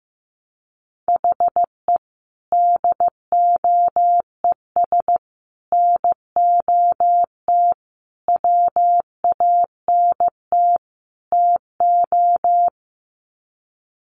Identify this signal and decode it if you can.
Morse code, heard as a keyed tone, about 15 wpm, 700 Hz. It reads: HE DOES NOT WANT TO
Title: Sets of 5 Words 15wpm